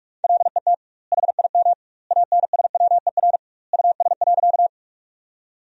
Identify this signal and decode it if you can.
Morse code, heard as a keyed tone, about 46 wpm, 700 Hz. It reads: LET HIM ANSWER US.
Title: 2x Word-Spacing - Sentences from Top 300 Words - 45wpm